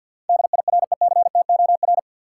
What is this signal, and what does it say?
Morse code, heard as a keyed tone, about 53 wpm, 700 Hz. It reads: DIRECTOR
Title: Sets of 3 Words - New Word Drill - 50wpm